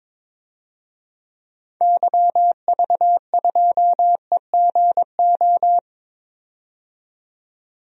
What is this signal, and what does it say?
Morse code, heard as a keyed tone, about 21 wpm, 700 Hz. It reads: YV2EGO
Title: DXCC-100 Call Signs - 6 Characters - 22wpm